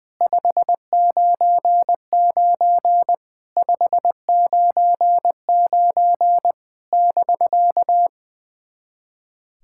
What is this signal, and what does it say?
Morse code, heard as a keyed tone, about 21 wpm, 700 Hz. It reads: 599 599 <BK>
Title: SOTA QSO Practice 20wpm